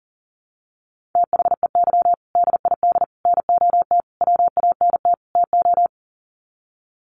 Morse code, heard as a keyed tone, about 40 wpm, 700 Hz. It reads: THEY DID NOT WANT TO
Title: Sets of 5 Words 40wpm